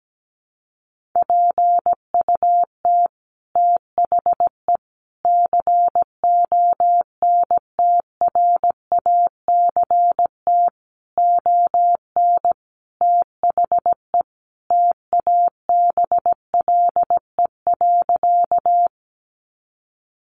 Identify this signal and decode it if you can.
Morse code, heard as a keyed tone, about 17 wpm, 700 Hz. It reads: PUT THE CONTRACT ON THE TABLE.
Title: Sentences from Top 1400 Words 17wpm